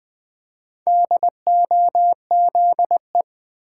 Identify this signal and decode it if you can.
Morse code, heard as a keyed tone, about 20 wpm, 700 Hz. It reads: DOZE
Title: Build A Word - 1 Letter At A Time - 3 to 5-Letter Words - 20wpm